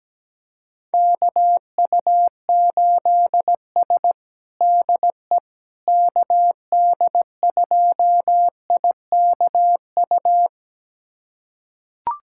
Morse code, heard as a keyed tone, about 17 wpm, 700 Hz. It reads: KU8S DE KD2IKU
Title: USA Callsign DE Callsign 17wpm